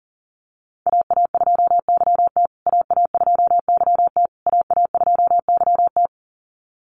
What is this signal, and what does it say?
Morse code, heard as a keyed tone, about 40 wpm, 700 Hz. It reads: AA2YT AA2YT AA2YT
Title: USA Call Signs Repeated 3x 40wpm